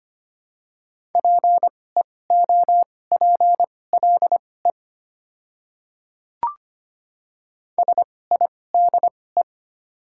Morse code, heard as a keyed tone, about 25 wpm, spaced as for 18 wpm, 690 Hz. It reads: PEOPLE SIDE
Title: 2x Character Spacing - Top 200 Words - 25wpm